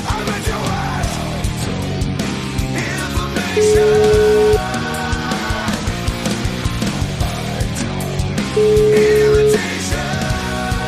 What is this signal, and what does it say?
Ringback tone (Europe, 425 Hz), with noise.